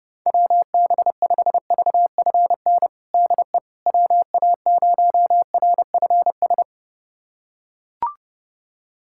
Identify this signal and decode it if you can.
Morse code, heard as a keyed tone, about 30 wpm, 700 Hz. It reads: WB5VFN DE WA0RFS
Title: USA Callsign DE Callsign - 6 Characters - 30wpm